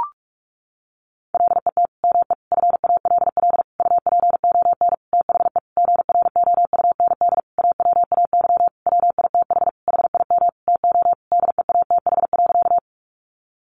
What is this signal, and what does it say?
Morse code, heard as a keyed tone, about 44 wpm, 700 Hz. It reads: LET ME FALL UPON THE GROUND AWAY WITH HIM TO DEATH.
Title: Sentences from Top 1100 Words 45wpm